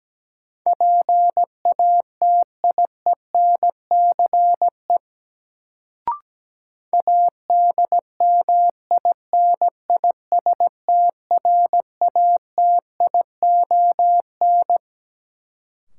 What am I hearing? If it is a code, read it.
Morse code, heard as a keyed tone, about 17 wpm, 700 Hz. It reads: PATIENCE ADMINISTRATION
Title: Sentences from Top 1300 Words - New Word Drill Alt - 17wpm